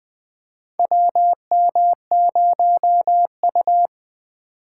Morse code, heard as a keyed tone, about 20 wpm, 700 Hz. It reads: WM0U
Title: USA Call Signs - 4 Characters - 4x Character Spacing - 20wpm